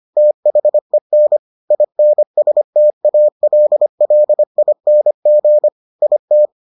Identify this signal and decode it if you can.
Morse code, heard as a keyed tone, about 25 wpm, 600 Hz. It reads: THEN INSTALLING IT